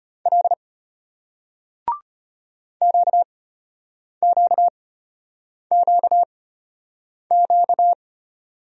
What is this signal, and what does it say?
Morse code, heard as a keyed tone, about 35 wpm, spaced as for 8 wpm, 700 Hz. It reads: L QQQQ